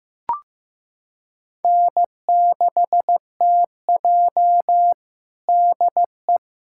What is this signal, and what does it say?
Morse code, heard as a keyed tone, about 15 wpm, 700 Hz. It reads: N6TJ DE
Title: USA Callsign DE Callsign - 4 Characters - 15wpm